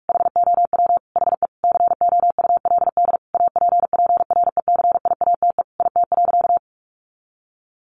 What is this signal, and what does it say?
Morse code, heard as a keyed tone, about 44 wpm, 700 Hz. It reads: HOW HE COULD APPRECIATE IT.
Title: Sentences from Top 1300 Words 45wpm